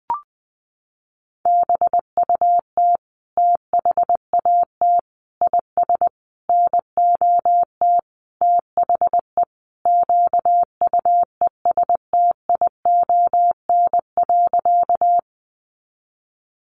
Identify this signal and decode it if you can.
Morse code, heard as a keyed tone, about 20 wpm, 700 Hz. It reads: BUT THAT IS NOT THE QUESTION.
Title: Sentences from Top 600 Words 20wpm